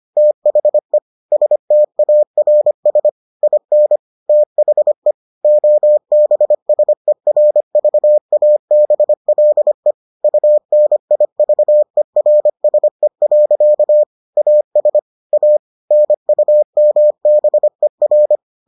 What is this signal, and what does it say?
Morse code, heard as a keyed tone, about 25 wpm, 600 Hz. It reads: THE STARS IN THE OBSERVABLE UNIVERSE. AS A NUMBER